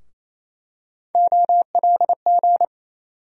Morse code, heard as a keyed tone, about 27 wpm, 700 Hz. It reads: OLG
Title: USA Call Sign Postfixes - 3 Letters 28wpm